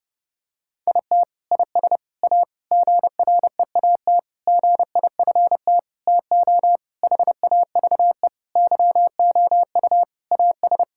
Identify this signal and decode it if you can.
Morse code, heard as a keyed tone, about 30 wpm, 700 Hz. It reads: IT IS A GREAT GIFT TO HAVE YOU AS
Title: Sentences from Top 1000 Words - Review Part 6 - 30wpm